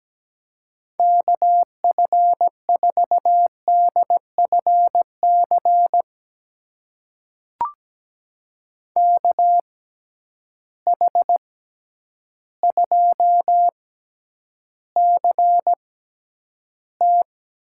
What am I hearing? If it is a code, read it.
Morse code, heard as a keyed tone, about 17 wpm, spaced as for 9 wpm, 700 Hz. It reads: KF4DFC KH2CT
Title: USA Call Signs - 6x Character Spacing - 17wpm